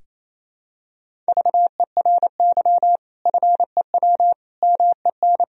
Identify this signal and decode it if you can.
Morse code, heard as a keyed tone, about 28 wpm, 700 Hz. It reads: VERY FEW MEN
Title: Sentences from Top 1000 Words - Review Part 1 - 28wpm